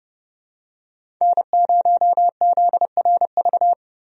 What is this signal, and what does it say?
Morse code, heard as a keyed tone, about 30 wpm, 700 Hz. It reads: N0ZRV
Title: USA Call Signs - 5 Characters - Repeated 3x - 30wpm